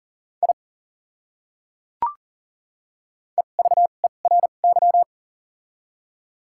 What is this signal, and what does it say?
Morse code, heard as a keyed tone, about 37 wpm, spaced as for 28 wpm, 700 Hz. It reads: I EVERY
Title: 2x Character Spacing - Top 200 Words - 40wpm